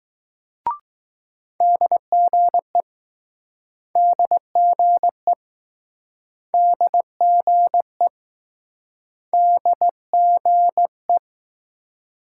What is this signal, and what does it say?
Morse code, heard as a keyed tone, about 21 wpm, 700 Hz. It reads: DGE DGE DGE DGE